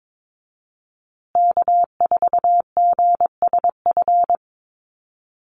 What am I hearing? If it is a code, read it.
Morse code, heard as a keyed tone, about 22 wpm, 700 Hz. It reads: K4GSF
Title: USA Call Signs Repeated 2x 22wpm